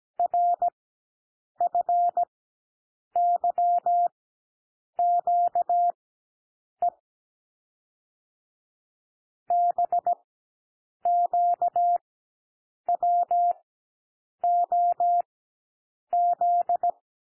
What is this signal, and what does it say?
Morse code, heard as a keyed tone, about 17 wpm, spaced as for 7 wpm, 700 Hz. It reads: RFYQE BQWOZ
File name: SOUND LETTERS 17 wpm